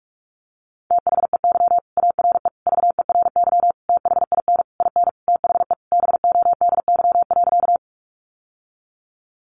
Morse code, heard as a keyed tone, about 44 wpm, 700 Hz. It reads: THEY ARE VERY THIN IN THE BODY.
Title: Sentences from Top 1200 Words 45wpm